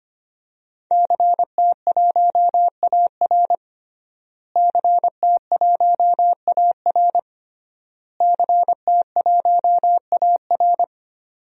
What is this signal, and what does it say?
Morse code, heard as a keyed tone, about 25 wpm, 700 Hz. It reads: CT1AR CT1AR CT1AR
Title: DXCC-100 Call Signs - 5 Characters - Repeated 3x - 25wpm